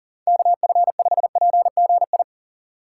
Morse code, heard as a keyed tone, about 40 wpm, 700 Hz. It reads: KF5PGI
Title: USA Call Signs - 2x Character Spacing - 40wpm